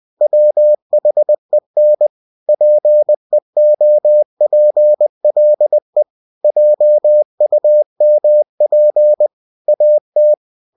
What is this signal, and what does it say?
Morse code, heard as a keyed tone, about 20 wpm, 600 Hz. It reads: WHEN PEOPLE JUMP AT
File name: winnie-20wpm